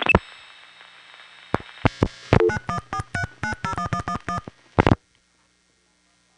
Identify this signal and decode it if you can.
DTMF, heard as telephone touch tones, tones of about 90 ms. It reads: 9143901111